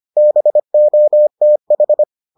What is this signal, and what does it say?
Morse code, heard as a keyed tone, about 25 wpm, 600 Hz. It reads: BOTH